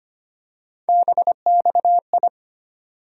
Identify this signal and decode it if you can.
Morse code, heard as a keyed tone, about 25 wpm, 700 Hz. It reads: BXI
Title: USA Call Sign Postfixes - 3 Letters 25wpm